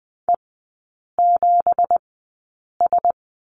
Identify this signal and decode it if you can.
Morse code, heard as a keyed tone, about 20 wpm, spaced as for 8 wpm, 700 Hz. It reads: E7S
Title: 184 Lesson 30e - Letter b - Three Characters 20wpm